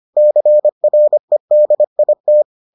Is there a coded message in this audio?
Yes: Morse code from 0.2 to 2.4 s, heard as a keyed tone, about 25 wpm, 600 Hz. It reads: CREDIT